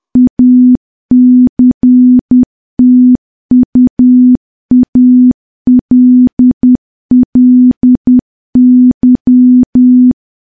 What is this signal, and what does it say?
Morse code, heard as a keyed tone, about 10 wpm, 250 Hz. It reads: ACTUALLY
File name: morse / morse-training-10WPM